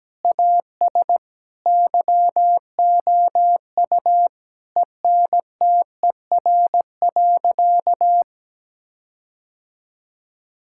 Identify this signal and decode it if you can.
Morse code, heard as a keyed tone, about 17 wpm, 700 Hz. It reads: AS YOU ENTER.
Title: Sentences from Top 1600 Words 17wpm